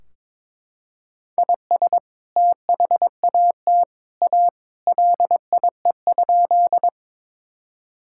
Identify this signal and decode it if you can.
Morse code, heard as a keyed tone, about 22 wpm, 700 Hz. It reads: IS THAT A LIE?
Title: Sentences from Top 1600 Words 22wpm